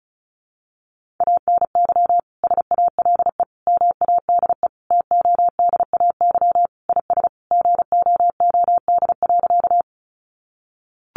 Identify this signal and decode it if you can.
Morse code, heard as a keyed tone, about 34 wpm, 700 Hz. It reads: ANY SALE MADE TODAY IS GOOD.
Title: Sentences from Top 1400 Words 35wpm